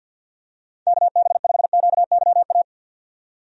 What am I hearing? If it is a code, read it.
Morse code, heard as a keyed tone, about 51 wpm, 700 Hz. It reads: KB5QYA
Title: USA Call Signs - 6 Characters - 50wpm